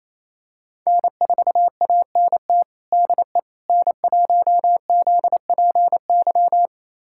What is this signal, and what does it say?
Morse code, heard as a keyed tone, about 27 wpm, 700 Hz. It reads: N4ANT DE N1ZPY
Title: USA Callsign DE Callsign - 5 Characters - 28wpm